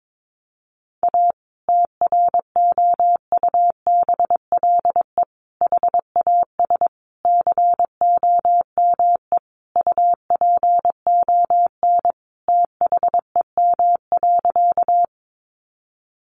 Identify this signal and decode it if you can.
Morse code, heard as a keyed tone, about 22 wpm, 700 Hz. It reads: A TROUBLE HAS COME UPON THEM.
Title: Sentences from Top 1200 Words 22wpm